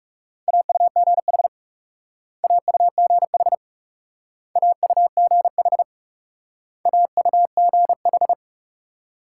Morse code, heard as a keyed tone, about 38 wpm, 700 Hz. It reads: AUGH AUGH AUGH AUGH
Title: Speed-Racing Multi-letter Phonemes - 30wpm